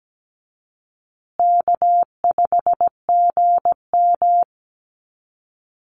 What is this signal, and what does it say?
Morse code, heard as a keyed tone, about 17 wpm, 700 Hz. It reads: K5GM
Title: Call Signs Repeated 3x - 17wpm